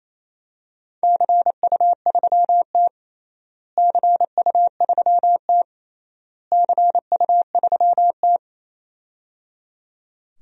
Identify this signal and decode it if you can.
Morse code, heard as a keyed tone, about 27 wpm, 700 Hz. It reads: CU3T CU3T CU3T
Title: DXCC-100 Call Signs - 4 Characters - Repeated 3x - 28wpm